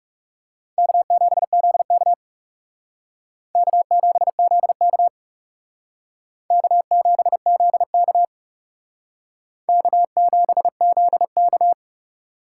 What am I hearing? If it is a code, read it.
Morse code, heard as a keyed tone, about 37 wpm, 700 Hz. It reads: K7ZK K7ZK K7ZK K7ZK
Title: Speed Racing - USA 1x2 Call Signs 30wpm